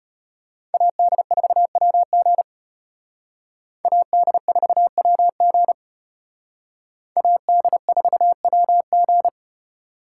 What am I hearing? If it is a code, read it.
Morse code, heard as a keyed tone, about 29 wpm, 700 Hz. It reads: AD4WG AD4WG AD4WG
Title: Speed Racing - USA 2x2 Call Signs 25wpm